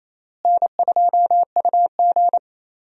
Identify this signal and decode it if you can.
Morse code, heard as a keyed tone, about 28 wpm, 700 Hz. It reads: N2UG
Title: USA Call Signs - 4 Characters - Repeated 3x - 28wpm